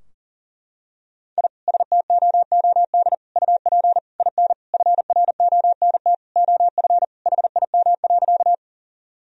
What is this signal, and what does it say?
Morse code, heard as a keyed tone, about 40 wpm, 700 Hz. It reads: I STOOD UP IN FRONT OF HIM.
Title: Sentences from Top 1000 Words - Review Part 2 - 40wpm